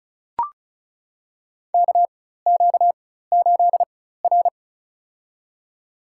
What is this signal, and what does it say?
Morse code, heard as a keyed tone, about 34 wpm, spaced as for 16 wpm, 700 Hz. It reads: KQ8R
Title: USA Call Signs - 4x Character Spacing - 35wpm